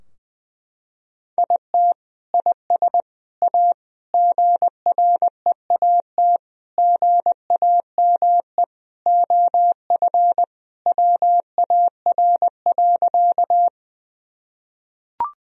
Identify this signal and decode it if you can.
Morse code, heard as a keyed tone, about 20 wpm, 700 Hz. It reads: IT IS A GREAT GAME OF WAR.
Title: Sentences from Top 1000 Words - Review Part 3 - 20wpm